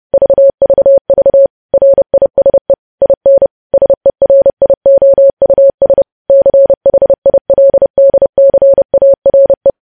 Morse code, heard as a keyed tone, about 30 wpm, 550 Hz. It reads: VVV RISE IN SERIOUS CHILDCARE
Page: news headlines in morse code at 30 wpm